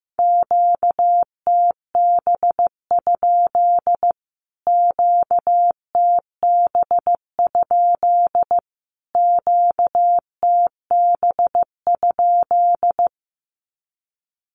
Morse code, heard as a keyed tone, about 15 wpm, 700 Hz. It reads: QTB? QTB? QTB?
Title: Q-Codes Repeated 3x 15wpm